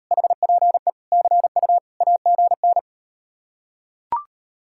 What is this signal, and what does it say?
Morse code, heard as a keyed tone, about 38 wpm, 700 Hz. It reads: HPE CU AGN